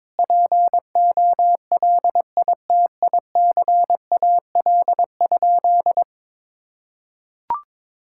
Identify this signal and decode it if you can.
Morse code, heard as a keyed tone, about 22 wpm, 700 Hz. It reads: POLITICAL?
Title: Sentences from Top 1000 Words - Review Part 5 - 22wpm